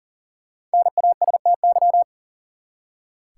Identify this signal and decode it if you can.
Morse code, heard as a keyed tone, about 37 wpm, 700 Hz. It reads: NASTY
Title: Five Letter Words - 40wpm